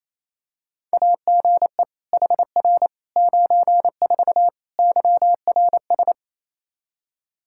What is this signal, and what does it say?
Morse code, heard as a keyed tone, about 28 wpm, 700 Hz. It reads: AGE HR 94 YRS
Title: Practice QSO - Exchange 3 - More Info - 28wpm